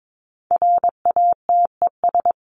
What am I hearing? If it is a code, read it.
Morse code, heard as a keyed tone, about 22 wpm, 700 Hz. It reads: RATES
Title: ES:S Suffix - Top 100 Words with Base Words 22wpm